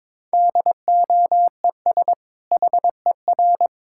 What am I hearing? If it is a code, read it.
Morse code, heard as a keyed tone, about 22 wpm, 700 Hz. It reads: DOES HER
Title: Sentences from Top 1000 Words - Review Part 5 - 22wpm